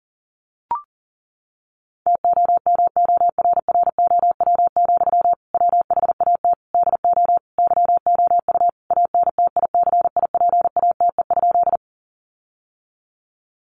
Morse code, heard as a keyed tone, about 37 wpm, 700 Hz. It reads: TOMORROW, WHAT DO YOU ANTICIPATE?
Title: Sentences from Top 1500 Words 40wpm